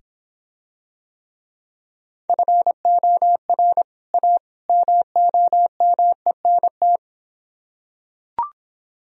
Morse code, heard as a keyed tone, about 26 wpm, 700 Hz. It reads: FOR A MOMENT